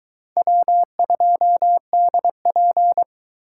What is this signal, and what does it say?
Morse code, heard as a keyed tone, about 22 wpm, 700 Hz. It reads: W2DP